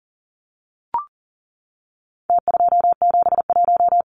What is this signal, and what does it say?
Morse code, heard as a keyed tone, about 40 wpm, 700 Hz. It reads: T271
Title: Numbers - Contest Style - 4-Digit 40wpm